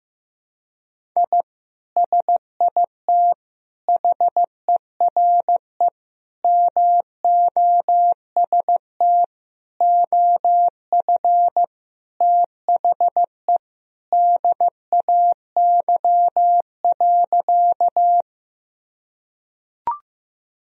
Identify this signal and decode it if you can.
Morse code, heard as a keyed tone, about 16 wpm, 700 Hz. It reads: I SIT HERE MOST OF THE DAY.
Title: Sentences from Top 1000 Words - Review Part 1 - 15wpm